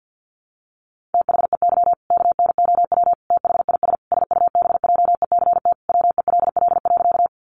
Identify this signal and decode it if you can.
Morse code, heard as a keyed tone, about 50 wpm, 700 Hz. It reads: THEY KNOW THIS SUBJECT WELL.
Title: Sentences from Top 800 Words 50wpm